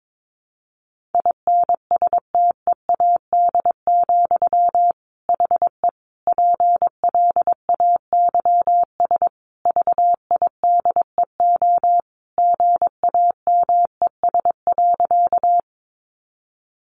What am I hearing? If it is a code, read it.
Morse code, heard as a keyed tone, about 22 wpm, 700 Hz. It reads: INSTEAD, HE PLAYS VIDEO GAMES.
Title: Additional Easy Phrases 22wpm